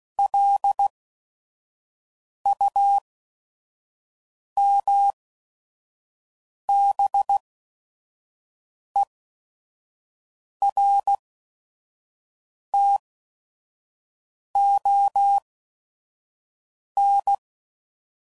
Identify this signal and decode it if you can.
Morse code, heard as a keyed tone, about 16 wpm, spaced as for 5 wpm, 800 Hz. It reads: LUMBERTON
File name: CW Code Groups @5WPM / fromSM's / Cities